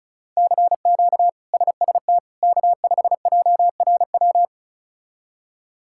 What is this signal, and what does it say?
Morse code, heard as a keyed tone, about 34 wpm, 700 Hz. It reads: CQ SST K5JRW
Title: K1USN SST QSO Practice - 35wpm